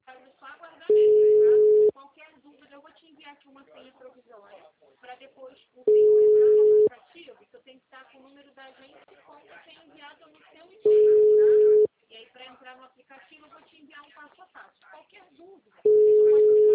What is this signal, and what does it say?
Ringback tone (Europe, 425 Hz), with speech.
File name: out- -2001- - - .75449